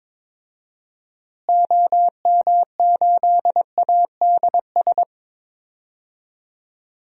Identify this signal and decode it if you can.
Morse code, heard as a keyed tone, about 22 wpm, 700 Hz. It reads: OM8ADS
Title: DXCC-100 Call Signs - 6 Characters - Repeated 3x - 22wpm